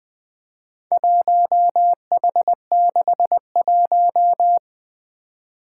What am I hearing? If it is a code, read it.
Morse code, heard as a keyed tone, about 20 wpm, 700 Hz. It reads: 1H61
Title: Four Challenging Letter-Number Combinations 20wpm